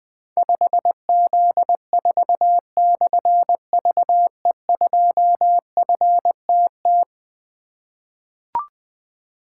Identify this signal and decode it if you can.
Morse code, heard as a keyed tone, about 20 wpm, 700 Hz. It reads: 5Z4/VE2FTT
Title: Difficult Call Signs 20wpm